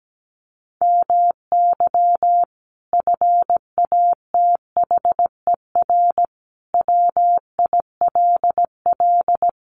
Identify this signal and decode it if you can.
Morse code, heard as a keyed tone, about 17 wpm, 700 Hz. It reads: MY FATHER WILL
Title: Sentences from Top 1100 Words 17wpm